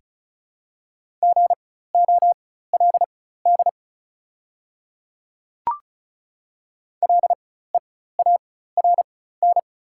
Morse code, heard as a keyed tone, about 34 wpm, spaced as for 16 wpm, 690 Hz. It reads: GOLD LEARN
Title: 4x Character Spacing - Top 500 Words - 35wpm